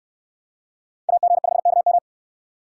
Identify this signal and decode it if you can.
Morse code, heard as a keyed tone, about 69 wpm, 700 Hz. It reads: AG5KM